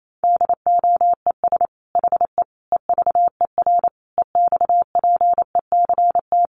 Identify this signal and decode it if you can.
Morse code, heard as a keyed tone, about 28 wpm, 700 Hz. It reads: DOES HE EVER EXPECT
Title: Sentences from Top 1100 Words 28wpm